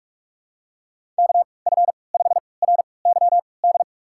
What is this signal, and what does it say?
Morse code, heard as a keyed tone, about 47 wpm, spaced as for 26 wpm, 700 Hz. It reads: KF5RYD
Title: USA Call Signs - 6 Characters - 3x Character Spacing - 45wpm